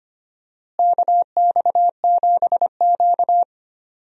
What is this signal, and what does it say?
Morse code, heard as a keyed tone, about 25 wpm, 700 Hz. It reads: KX7Q
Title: USA Call Signs - 4 Characters - Repeated 3x - 25wpm